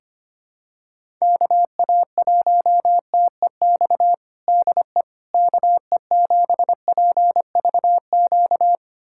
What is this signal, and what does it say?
Morse code, heard as a keyed tone, about 25 wpm, 700 Hz. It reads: KA1TEX DE KE7PVQ
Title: USA Callsign DE Callsign 25wpm